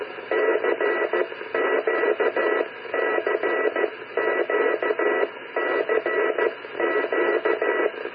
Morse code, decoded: CQ CQ CQ